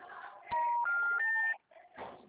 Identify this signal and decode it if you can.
Special information tone, with noise.
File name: out- -2021- - - .774